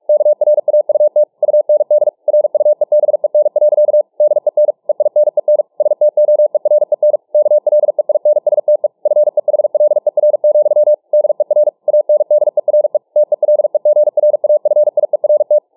Morse code, heard as a keyed tone, about 47 wpm, 600 Hz. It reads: KRAUT UND RUEBEN. DEN EINEN STOEREN KLEINSTE FEHLER, DER ANDERE TELEGRAFIERT